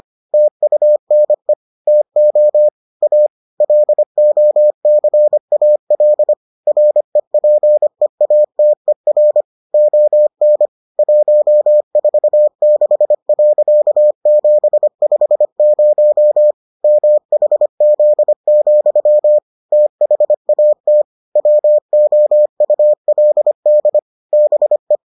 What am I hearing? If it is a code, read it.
Morse code, heard as a keyed tone, about 25 wpm, 600 Hz. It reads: TUNE TO A LOCAL REPEATER ON 146.750 MHZ, THAT WOULD BE